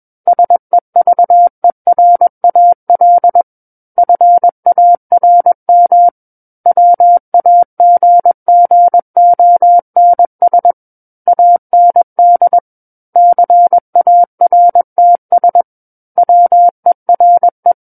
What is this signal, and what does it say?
Morse code, heard as a keyed tone, about 21 wpm, 700 Hz. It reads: SEVERAL FARM WAGGONS AND CARTS WERE
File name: Book One / 12-Chapter 12 - 21 WPM